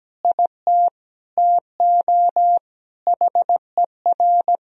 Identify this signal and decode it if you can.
Morse code, heard as a keyed tone, about 17 wpm, 700 Hz. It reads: IT TO HER
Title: Sentences from Top 1000 Words - Review Part 4 - 17wpm